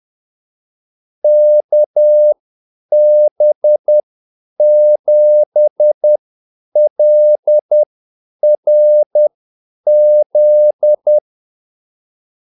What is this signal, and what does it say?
Morse code, heard as a keyed tone, about 10 wpm, spaced as for 8 wpm, 600 Hz. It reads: KB7LRZ
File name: Callsigns-10-8wpm